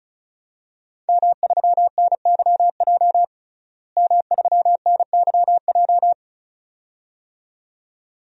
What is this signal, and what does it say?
Morse code, heard as a keyed tone, about 36 wpm, 700 Hz. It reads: M3NYJ M3NYJ
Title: DXCC-100 Call Signs - 5 Characters - Repeated 2x - 35wpm